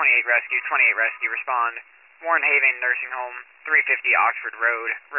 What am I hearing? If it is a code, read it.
Emergency Alert System (SAME) tones, heard as harsh buzzing bursts, with speech. It is heard from 0.9 to 4.8 s.